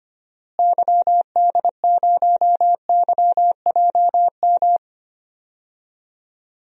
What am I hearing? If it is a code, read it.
Morse code, heard as a keyed tone, about 25 wpm, 700 Hz. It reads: YD0YJM
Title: DXCC-100 Call Signs - 6 Characters - Repeated 3x - 25wpm